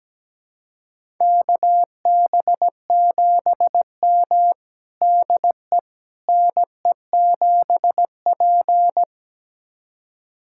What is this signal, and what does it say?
Morse code, heard as a keyed tone, about 17 wpm, 700 Hz. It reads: KB7M DE NE7P
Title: USA Callsign DE Callsign - 4 Characters - 17wpm